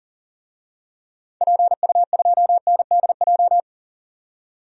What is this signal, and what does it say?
Morse code, heard as a keyed tone, about 40 wpm, 700 Hz. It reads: PU2NDJ
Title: DXCC-100 Call Signs - 6 Characters - 40wpm